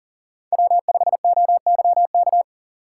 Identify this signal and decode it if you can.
Morse code, heard as a keyed tone, about 40 wpm, 700 Hz. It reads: W5OYK
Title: USA Call Signs - 5 Characters - 3x Character Spacing - 40wpm